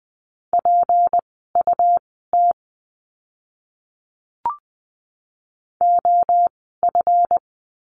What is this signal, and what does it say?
Morse code, heard as a keyed tone, about 20 wpm, spaced as for 15 wpm, 700 Hz. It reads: PUT OF
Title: 237 Lesson 38f - Letter q - Words 20wpm